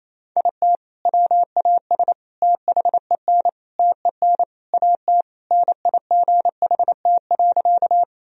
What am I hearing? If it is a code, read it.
Morse code, heard as a keyed tone, about 28 wpm, 700 Hz. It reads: IT WAS THEN TEN AT NIGHT.
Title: Sentences from Top 1000 Words - Review Part 2 - 28wpm